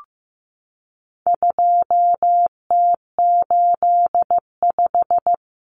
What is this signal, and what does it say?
Morse code, heard as a keyed tone, about 15 wpm, 700 Hz. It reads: 2T85
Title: Common Cut Numbers - 4-Digit 15wpm